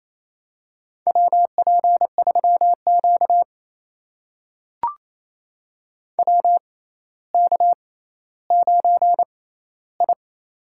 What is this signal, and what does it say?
Morse code, heard as a keyed tone, about 28 wpm, spaced as for 14 wpm, 700 Hz. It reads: WP3Q WK9I
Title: USA Call Signs - 4 Characters - 6x Character Spacing - 28wpm